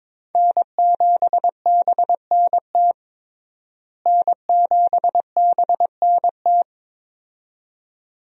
Morse code, heard as a keyed tone, about 22 wpm, 700 Hz. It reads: N7BNT N7BNT
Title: USA Call Signs - 5 Characters - Repeated 2x - 22wpm